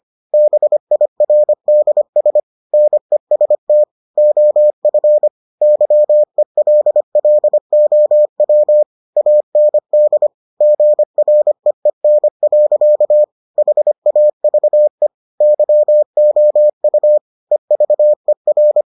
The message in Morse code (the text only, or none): BIRDS NEST OF YELLOW AND GREEN. HAVE YOU EVER